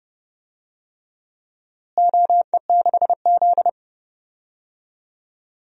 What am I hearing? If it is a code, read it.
Morse code, heard as a keyed tone, about 31 wpm, 700 Hz. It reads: OE6Z
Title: DXCC-100 Call Signs - 4 Characters - Repeated 3x - 30wpm